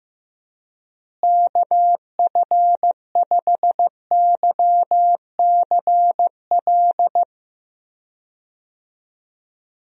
Morse code, heard as a keyed tone, about 16 wpm, 700 Hz. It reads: KF5YCL
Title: USA Call Signs - 6 Characters - Repeated 3x - 15wpm